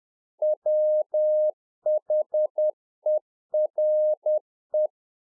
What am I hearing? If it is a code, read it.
Morse code, heard as a keyed tone, about 10 wpm, 610 Hz. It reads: WHERE